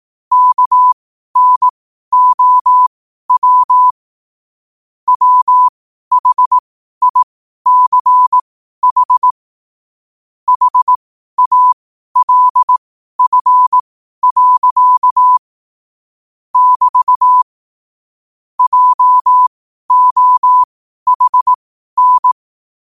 Morse code, heard as a keyed tone, about 18 wpm, spaced as for 12 wpm, 1000 Hz. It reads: KNOW WHICH HALF. <BT> JOHN
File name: qotdfiles / 12wpm